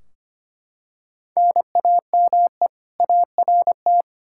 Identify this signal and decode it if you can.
Morse code, heard as a keyed tone, about 25 wpm, 700 Hz. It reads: NAME ART
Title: K1USN SST QSO Practice 25wpm